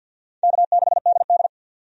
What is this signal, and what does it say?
Morse code, heard as a keyed tone, about 52 wpm, 700 Hz. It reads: K6DD
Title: USA Call Signs - 4 Characters - 50wpm